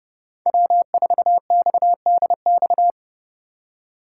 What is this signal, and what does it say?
Morse code, heard as a keyed tone, about 36 wpm, 700 Hz. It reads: W4XDX
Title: Speed Racing - USA 1x3 Call Signs 30wpm